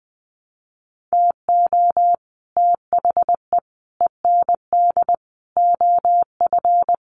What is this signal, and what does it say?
Morse code, heard as a keyed tone, about 20 wpm, 700 Hz. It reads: TO THE END OF
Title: Sets of 4 Words 20wpm